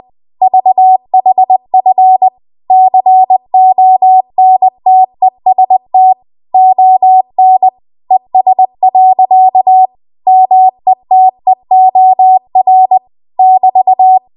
Morse code, heard as a keyed tone, about 20 wpm, 750 Hz. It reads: VHF CONTEST ON ES. METEOR <BT>
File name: Archive / 20 WPM / 20WPM